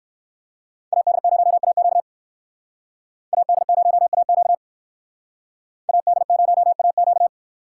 Morse code, heard as a keyed tone, about 56 wpm, 700 Hz. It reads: AD0AQ AD0AQ AD0AQ